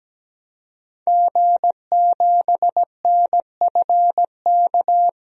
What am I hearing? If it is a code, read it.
Morse code, heard as a keyed tone, about 17 wpm, 700 Hz. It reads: G7NFK
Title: DXCC-100 Call Signs - 5 Characters - Repeated 3x - 17wpm